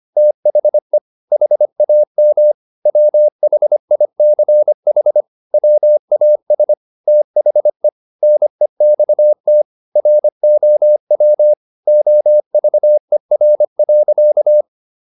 Morse code, heard as a keyed tone, about 25 wpm, 600 Hz. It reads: THE HAM WHICH WAS THE NEXT ROW OVER.